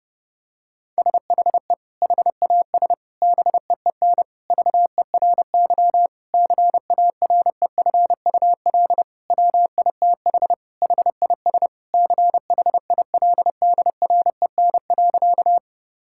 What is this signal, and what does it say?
Morse code, heard as a keyed tone, about 30 wpm, 700 Hz. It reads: SHE HAS BEEN VERY CAREFUL WITH HIS CHILDREN.
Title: Sentences from Top 1600 Words 30wpm